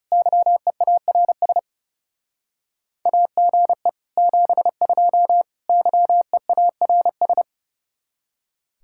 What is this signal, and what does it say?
Morse code, heard as a keyed tone, about 37 wpm, 700 Hz. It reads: YEARS AGE 72 YEARS
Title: Speed-Racing Common QSO Phrases Encore - 30wpm